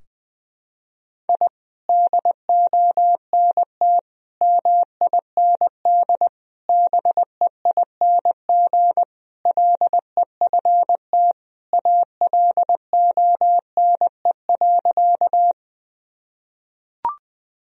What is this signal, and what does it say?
Morse code, heard as a keyed tone, about 21 wpm, 700 Hz. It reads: I DONT MIND BEING LEFT ALONE.
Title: Sentences from Top 1000 Words - Review Part 3 - 20wpm